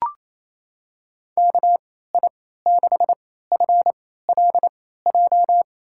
Morse code, heard as a keyed tone, about 27 wpm, spaced as for 16 wpm, 700 Hz. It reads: KI6FLJ
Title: USA Call Signs - 6 Characters - 3x Character Spacing - 28wpm